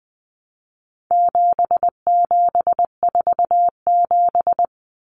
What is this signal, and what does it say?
Morse code, heard as a keyed tone, about 20 wpm, 700 Hz. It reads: 7747
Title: Common Cut Numbers - 4-Digit 20wpm